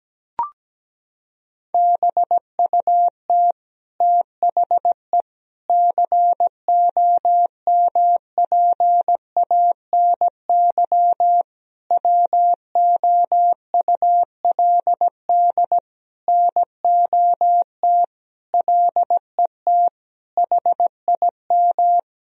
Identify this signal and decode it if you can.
Morse code, heard as a keyed tone, about 17 wpm, 700 Hz. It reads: BUT THE COMPANY WOULD NOT LET HIM
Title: Sentences from Top 1000 Words - Review Part 3 - 17wpm